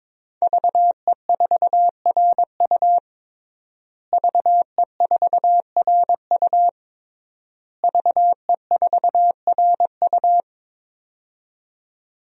Morse code, heard as a keyed tone, about 21 wpm, 700 Hz. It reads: VE4RU VE4RU VE4RU
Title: DXCC-100 Call Signs - 5 Characters - Repeated 3x - 22wpm